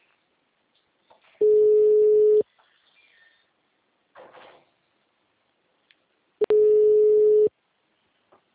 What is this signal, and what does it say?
Ringback tone (Europe, 425 Hz).